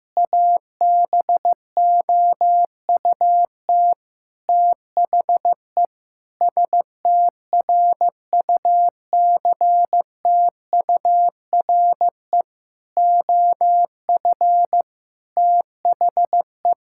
Morse code, heard as a keyed tone, about 16 wpm, 700 Hz. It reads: ABOUT THE STRUCTURE OF THE
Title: Sentences from Top 1000 Words - Review Part 6 - 15wpm